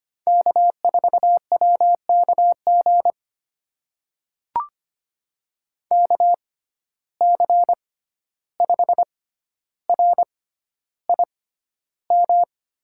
Morse code, heard as a keyed tone, about 25 wpm, spaced as for 12 wpm, 700 Hz. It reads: K4WKG KC5RIM
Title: USA Call Signs - 6x Character Spacing - 25wpm